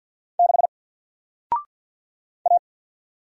Morse code, heard as a keyed tone, about 50 wpm, 700 Hz. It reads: 6 A
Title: Single Letter-Number 50wpm